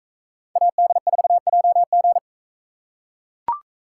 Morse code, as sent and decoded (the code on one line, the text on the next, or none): .- -.. ....- .--- --.
AD4JG